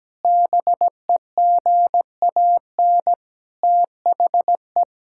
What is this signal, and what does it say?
Morse code, heard as a keyed tone, about 17 wpm, 700 Hz. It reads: BEGAN THE
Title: Sentences from Top 1000 Words - Review Part 5 - 17wpm